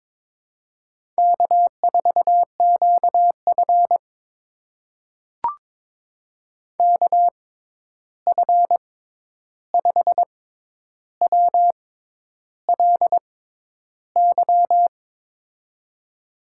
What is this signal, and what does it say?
Morse code, heard as a keyed tone, about 22 wpm, spaced as for 10 wpm, 700 Hz. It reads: K4QF KF5WLY
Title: USA Call Signs - 6x Character Spacing - 22wpm